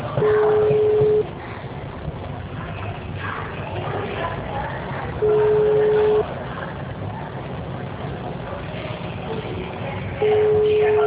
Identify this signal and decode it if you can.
Ringback tone (Europe, 425 Hz), with noise.
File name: out- -2014- - - .20677